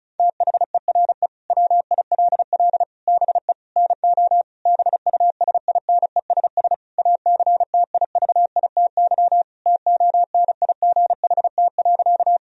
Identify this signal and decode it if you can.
Morse code, heard as a keyed tone, about 36 wpm, 700 Hz. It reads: THERE WILL BE NO BUSINESS ACTIVITY TONIGHT.
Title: Sentences from Top 1000 Words - Review Part 4 - 35wpm